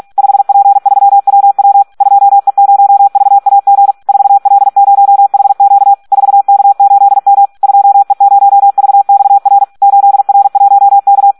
Morse code, heard as a keyed tone, about 46 wpm, 800 Hz. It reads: 5P3WW 2E0VAG 4L0HQ 4K8M 2E0VXR 8R1K
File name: Audio practica CW / SD 1 46